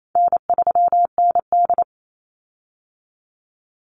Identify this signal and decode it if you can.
Morse code, heard as a keyed tone, about 28 wpm, 700 Hz. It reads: N3ND
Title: Easy Call Signs Repeated 3x - 28wpm